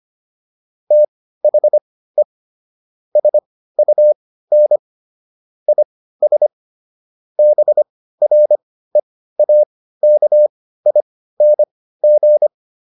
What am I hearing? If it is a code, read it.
Morse code, heard as a keyed tone, about 25 wpm, spaced as for 15 wpm, 600 Hz. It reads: THE SUN IS BREAKING